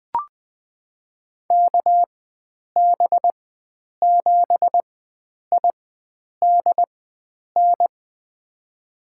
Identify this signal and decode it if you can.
Morse code, heard as a keyed tone, about 20 wpm, spaced as for 9 wpm, 700 Hz. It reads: KB7IDN
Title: USA Call Signs - 6 Characters - 4x Character Spacing - 20wpm